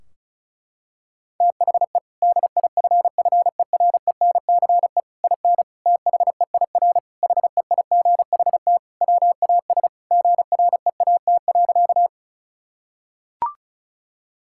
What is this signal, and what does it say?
Morse code, heard as a keyed tone, about 36 wpm, 700 Hz. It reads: THE DIFFERENCE IN THEIR HEIGHT WAS GREAT.
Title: Sentences from Top 1600 Words 35wpm